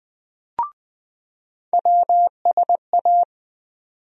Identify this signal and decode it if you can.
Morse code, heard as a keyed tone, about 20 wpm, 700 Hz. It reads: WSA
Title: USA Call Sign Postfixes - 3 Letters 20wpm